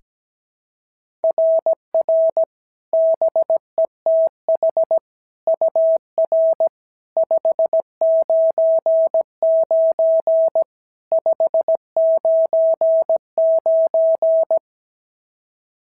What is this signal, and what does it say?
Morse code, heard as a keyed tone, about 17 wpm, 690 Hz. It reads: RR BETH UR 599 599
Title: SOTA QSO Practice 17wpm